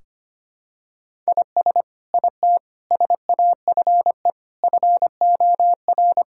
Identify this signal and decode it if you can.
Morse code, heard as a keyed tone, about 25 wpm, 700 Hz. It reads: IS IT SAFE FOR
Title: Sentences from Top 1600 Words 25wpm